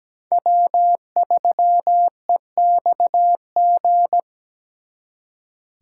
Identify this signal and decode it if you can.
Morse code, heard as a keyed tone, about 20 wpm, 700 Hz. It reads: W3EXG
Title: Speed Racing - USA 1x3 Call Signs 17wpm